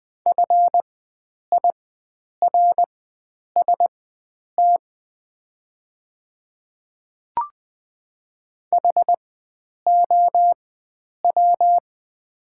Morse code, heard as a keyed tone, about 20 wpm, spaced as for 9 wpm, 700 Hz. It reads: FIRST HOW
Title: 4x Character Spacing - Top 100 Words - 20wpm